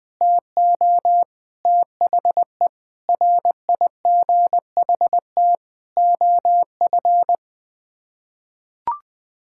Morse code, heard as a keyed tone, about 21 wpm, 700 Hz. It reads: TO THE RIGHT OF
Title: Sets of 8 Words - 20wpm